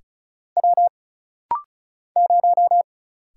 Morse code, heard as a keyed tone, about 36 wpm, 700 Hz. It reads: W 0
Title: Single Letter-Number Alt - Rapid-Fire 35wpm